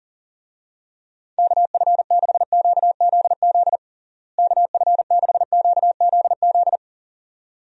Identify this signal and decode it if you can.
Morse code, heard as a keyed tone, about 40 wpm, 700 Hz. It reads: KF6QZZ KF6QZZ
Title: USA Call Signs - 6 Characters - Repeated 2x - 40wpm